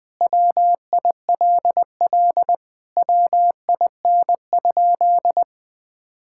Morse code, heard as a keyed tone, about 20 wpm, 700 Hz. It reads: WILL WIN?
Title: Sentences with 4-Word Phrases - 20wpm